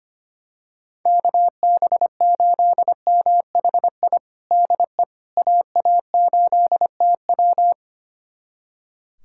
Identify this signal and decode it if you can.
Morse code, heard as a keyed tone, about 25 wpm, 700 Hz. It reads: KB8MHI DE AA8TW
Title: USA Callsign DE Callsign 25wpm